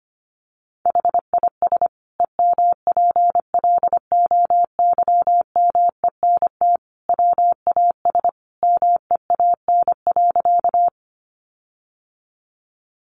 Morse code, heard as a keyed tone, about 25 wpm, 700 Hz. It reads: HIS EMPLOYMENT WAS MEAN.
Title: Sentences from Top 1500 Words 25wpm